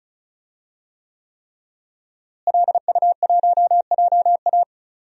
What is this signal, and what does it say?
Morse code, heard as a keyed tone, about 35 wpm, 700 Hz. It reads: LU1JA
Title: DXCC-100 Call Signs - 5 Characters - Repeated 3x - 35wpm